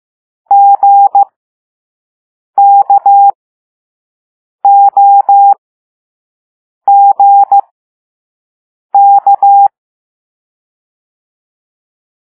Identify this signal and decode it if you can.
Morse code, heard as a keyed tone, about 15 wpm, spaced as for 5 wpm, 800 Hz. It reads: GKOGK
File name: Morse / Level4A